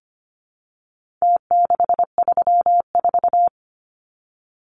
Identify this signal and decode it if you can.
Morse code, heard as a keyed tone, about 25 wpm, 700 Hz. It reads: T634
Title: Numbers - Contest Style - 4-Digit 25wpm